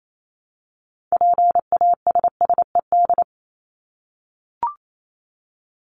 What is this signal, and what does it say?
Morse code, heard as a keyed tone, about 28 wpm, 700 Hz. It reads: PASSED
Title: ED Suffix - Top 100 Words with Base Words 28wpm